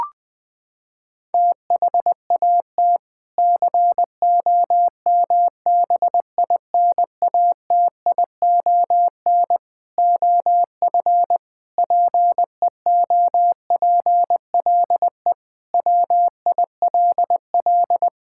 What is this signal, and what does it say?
Morse code, heard as a keyed tone, about 20 wpm, 700 Hz. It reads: THAT COMBINATION OF PEOPLE WILL
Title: Sentences from Top 1000 Words - Review Part 6 - 20wpm